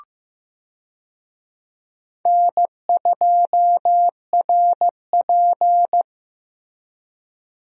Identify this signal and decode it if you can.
Morse code, heard as a keyed tone, about 16 wpm, 700 Hz. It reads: N2RP
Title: USA Call Signs - 4 Characters - Repeated 3x - 15wpm